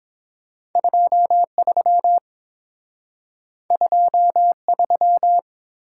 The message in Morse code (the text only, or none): 2 3 2 3